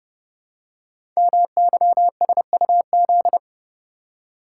Morse code, heard as a keyed tone, about 28 wpm, 700 Hz. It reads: MYSUZ
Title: Code Groups - 5 Letters - 30wpm